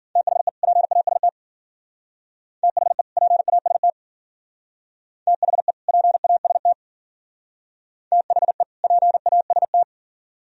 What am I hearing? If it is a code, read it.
Morse code, heard as a keyed tone, about 47 wpm, 700 Hz. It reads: THE PAST THE PAST THE PAST THE PAST
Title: Speed-Racing - Sets of 2 Words 40wpm